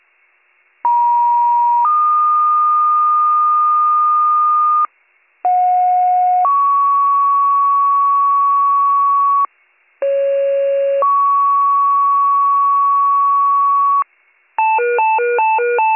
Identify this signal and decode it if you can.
Two-tone page, heard as a one-second tone, then a three-second tone, about 950 then 1220 Hz, about 730 then 1120 Hz, about 550 then 1120 Hz. It is heard from 0.8 to 14.0 s.